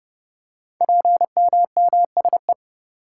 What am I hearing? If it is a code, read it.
Morse code, heard as a keyed tone, about 28 wpm, 700 Hz. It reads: PMMSE
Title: Code Groups - 5 Letters - 30wpm